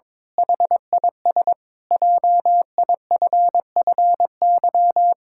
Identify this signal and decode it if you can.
Morse code, heard as a keyed tone, about 22 wpm, 700 Hz. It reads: HIS JIFFY
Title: Pangram Challenge v2 - 22wpm